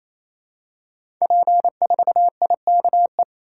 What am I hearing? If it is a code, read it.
Morse code, heard as a keyed tone, about 28 wpm, 700 Hz. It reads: P4IKE
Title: Code Groups - 5 Letter:Numbers - 28wpm